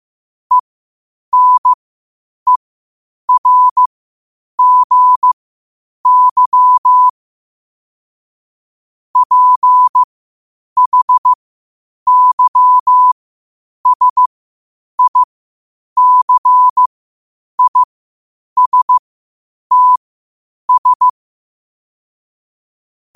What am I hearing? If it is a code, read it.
Morse code, heard as a keyed tone, about 15 wpm, spaced as for 8 wpm, 1000 Hz. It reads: ENERGY PHYSICISTS
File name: qotdfiles / 8wpm